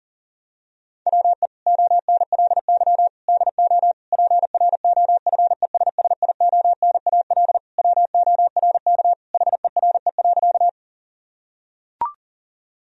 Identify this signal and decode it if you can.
Morse code, heard as a keyed tone, about 40 wpm, 700 Hz. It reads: WE ONLY DO PROFESSIONAL WORK HERE.
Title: Sentences from Top 1000 Words - Review Part 4 - 40wpm